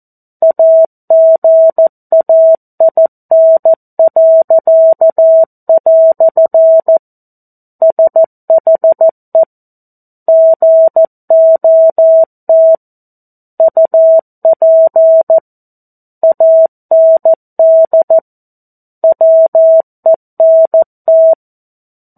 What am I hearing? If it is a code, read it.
Morse code, heard as a keyed tone, about 14 wpm, 650 Hz. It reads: AGAIN." SHE GOT UP AND WENT
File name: 02-Chapter-02 14 WPM